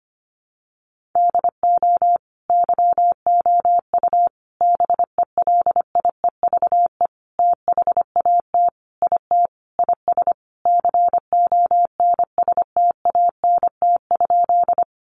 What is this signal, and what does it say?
Morse code, heard as a keyed tone, about 25 wpm, 700 Hz. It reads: DO YOU BELIEVE THAT IT IS CONSTANT?
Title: Sentences from Top 1500 Words 25wpm